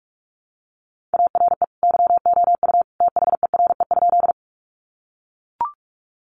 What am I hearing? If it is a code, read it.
Morse code, heard as a keyed tone, about 44 wpm, 700 Hz. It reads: ARE YOU THERE?
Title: Sentences from Top 100 Words 45wpm